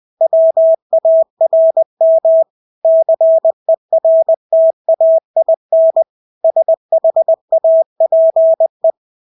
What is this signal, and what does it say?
Morse code, heard as a keyed tone, about 20 wpm, 650 Hz. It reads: WARM CERTAIN SHAPE
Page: Engelske ord 20 wpm